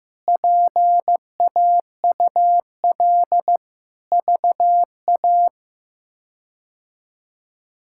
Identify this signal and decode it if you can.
Morse code, heard as a keyed tone, about 16 wpm, 700 Hz. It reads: PAUL VA
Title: K1USN SST QSO Practice - 15wpm